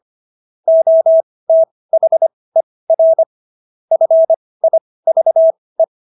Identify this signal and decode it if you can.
Morse code, heard as a keyed tone, about 25 wpm, spaced as for 18 wpm, 650 Hz. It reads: OTHER FIVE